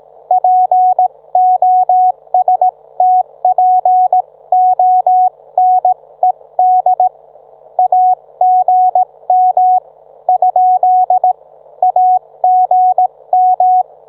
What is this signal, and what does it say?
Morse code, heard as a keyed tone, about 18 wpm, 740 Hz. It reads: POSTPONED AGM ? AGM